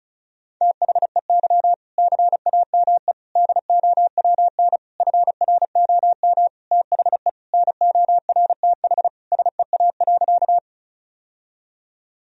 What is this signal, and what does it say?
Morse code, heard as a keyed tone, about 34 wpm, 700 Hz. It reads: THEY CAME DOWN FROM THE NORTH SEA.
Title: Sentences from Top 1000 Words - Review Part 1 - 35wpm